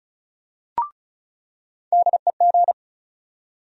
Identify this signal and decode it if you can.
Morse code, heard as a keyed tone, about 36 wpm, 700 Hz. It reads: DEG
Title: USA Call Sign Postfixes - 3 Letters 35wpm